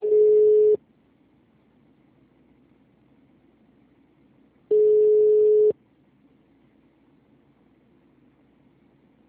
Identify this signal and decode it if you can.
Ringback tone (Europe, 425 Hz).